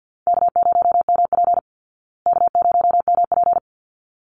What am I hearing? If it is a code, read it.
Morse code, heard as a keyed tone, about 52 wpm, 700 Hz. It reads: K0MP K0MP
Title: Call Signs Repeated 3x - 50wpm